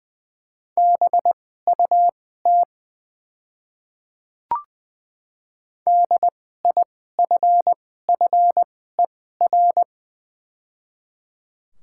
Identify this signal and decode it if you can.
Morse code, heard as a keyed tone, about 20 wpm, spaced as for 15 wpm, 700 Hz. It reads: BUT DIFFER
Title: 2x Character Spacing - Top 300 Words - 20wpm